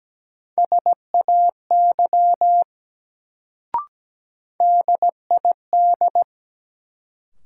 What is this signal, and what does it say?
Morse code, heard as a keyed tone, about 17 wpm, 700 Hz. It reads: SAY DID
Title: Sets of 2 Words - New Word Drill - 17wpm